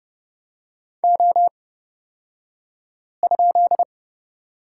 Morse code, heard as a keyed tone, about 30 wpm, spaced as for 5 wpm, 700 Hz. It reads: O?
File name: course / 30:10wpm / 235 Lesson 38d - Letter q - Two Characters 30wpm